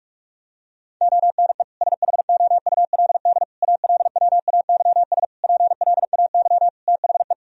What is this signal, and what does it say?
Morse code, heard as a keyed tone, about 46 wpm, 700 Hz. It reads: ONE SHOULD ALWAYS PLAY THE
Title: Sentences from Top 1000 Words - Review Part 3 - 45wpm